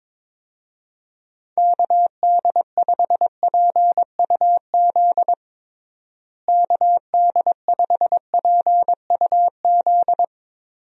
Morse code, heard as a keyed tone, about 22 wpm, 700 Hz. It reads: KD5PUZ KD5PUZ
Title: USA Call Signs - 6 Characters - Repeated 2x - 22wpm